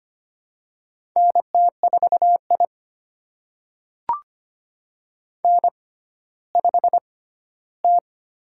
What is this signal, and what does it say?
Morse code, heard as a keyed tone, about 25 wpm, spaced as for 14 wpm, 700 Hz. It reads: NT4I N5T
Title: USA Call Signs - 4 Characters - 6x Character Spacing - 25wpm